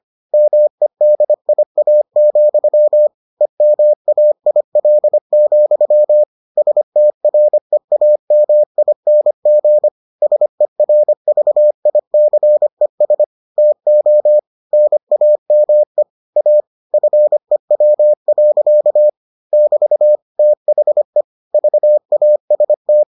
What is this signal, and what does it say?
Morse code, heard as a keyed tone, about 25 wpm, 600 Hz. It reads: MEDIA, EMAIL, STREAMING SERVICES TO NAME A FEW. <BT> THE VAST